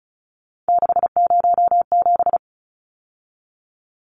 Morse code, heard as a keyed tone, about 34 wpm, 700 Hz. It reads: 607
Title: Numbers - 3-Digit 35wpm